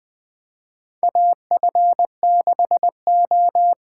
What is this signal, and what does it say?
Morse code, heard as a keyed tone, about 20 wpm, 700 Hz. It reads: AF6O
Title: USA Call Signs - 6x Character Spacing - 20wpm